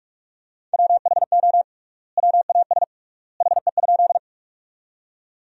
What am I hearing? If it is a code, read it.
Morse code, heard as a keyed tone, about 44 wpm, 700 Hz. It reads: WHO WAS HE?
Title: 3x Word-Spacing - Sentences from Top 100 Words - 45wpm